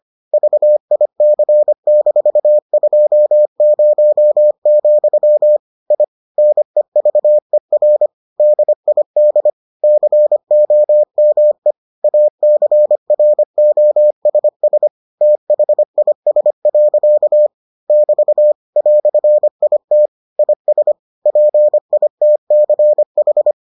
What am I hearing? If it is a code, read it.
Morse code, heard as a keyed tone, about 25 wpm, 600 Hz. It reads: VIC-20, I NEVER DID COME ACROSS THIS. <BT> "IT IS PITCH